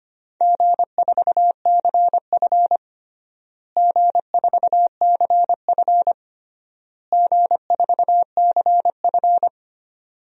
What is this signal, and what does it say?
Morse code, heard as a keyed tone, about 25 wpm, 700 Hz. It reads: G4CF G4CF G4CF
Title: DXCC-100 Call Signs - 4 Characters - Repeated 3x - 25wpm